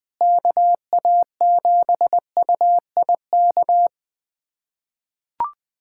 Morse code, heard as a keyed tone, about 20 wpm, 700 Hz. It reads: KA7UIK
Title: USA Callsign DE Callsign - 6 Characters - 20wpm